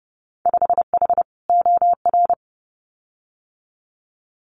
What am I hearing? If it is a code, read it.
Morse code, heard as a keyed tone, about 30 wpm, 700 Hz. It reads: 5H OR
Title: Winter Field Day Exchanges Encore 30wpm